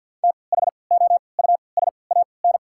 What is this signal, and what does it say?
Morse code, heard as a keyed tone, about 48 wpm, spaced as for 28 wpm, 700 Hz. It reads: THOUSAN
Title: 3x Character Spacing - Top 500 Words - 50wpm